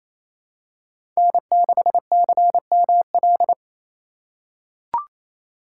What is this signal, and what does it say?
Morse code, heard as a keyed tone, about 28 wpm, 700 Hz. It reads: N6CML
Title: USA Call Signs - 5 Characters - 4x Character Spacing - 28wpm